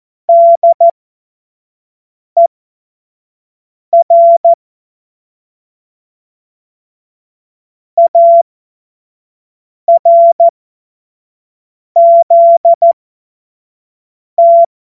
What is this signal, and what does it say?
Morse code, heard as a keyed tone, about 14 wpm, spaced as for 5 wpm, 680 Hz. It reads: DER ARZT
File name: Morsekurs / Witze / Witz 14.November 2023 25 F70